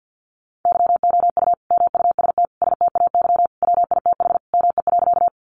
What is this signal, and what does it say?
Morse code, heard as a keyed tone, about 50 wpm, 700 Hz. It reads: YOU MUST STAY WITH ME.
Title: Sentences from Top 500 Words 50wpm